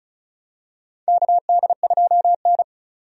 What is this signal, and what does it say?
Morse code, heard as a keyed tone, about 36 wpm, 700 Hz. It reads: KD2N
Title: USA Call Signs - 4 Characters - 35wpm